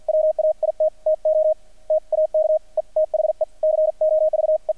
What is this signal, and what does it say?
Morse code, heard as a keyed tone, about 50 wpm, spaced as for 42 wpm, 620 Hz, with radio noise. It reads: WAIT TO TAKE THE COVE
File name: archived / CRSnet